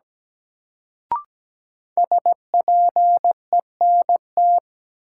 Morse code, heard as a keyed tone, about 17 wpm, 700 Hz. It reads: SPENT
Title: Sets of 5 Words - New Word Drill - 17wpm